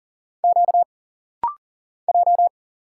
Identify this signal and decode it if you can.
Morse code, heard as a keyed tone, about 40 wpm, 700 Hz. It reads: Q J
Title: Single Letters - Rapid-Fire 40wpm